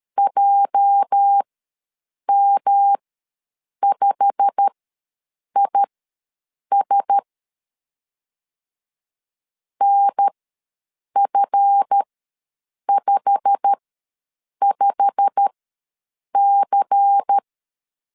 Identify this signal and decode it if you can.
Morse code, heard as a keyed tone, about 13 wpm, spaced as for 7 wpm, 800 Hz. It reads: JM5IS NF55C
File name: LEZIONI / 8 YJ45 / 8 YJ45 ws 12wpm 3146